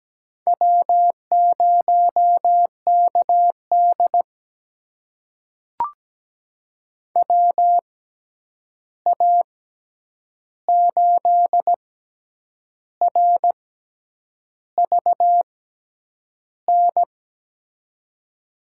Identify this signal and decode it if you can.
Morse code, heard as a keyed tone, about 17 wpm, spaced as for 8 wpm, 700 Hz. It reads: W0KD WA8RVN
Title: USA Call Signs - 6x Character Spacing - 17wpm